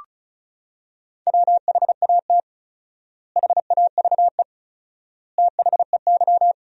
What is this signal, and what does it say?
Morse code, heard as a keyed tone, about 34 wpm, 700 Hz. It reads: WHAT HAVE THEY
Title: 4x Word-Spacing - Sentences from Top 500 Words - 35wpm